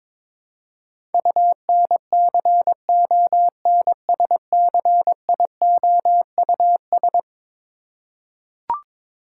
Morse code, heard as a keyed tone, about 21 wpm, 700 Hz. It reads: UNCONSCIOUS
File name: practice / 22wpm / Eleven Letter Words - 22wpm